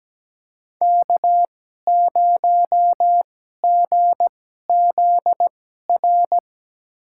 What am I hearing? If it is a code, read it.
Morse code, heard as a keyed tone, about 17 wpm, spaced as for 12 wpm, 700 Hz. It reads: K0GZR
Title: USA Call Signs - 5 Characters - 2x Character Spacing - 17wpm